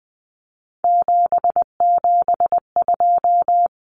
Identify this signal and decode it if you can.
Morse code, heard as a keyed tone, about 20 wpm, 700 Hz. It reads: 772
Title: Numbers - 3-Digit 20wpm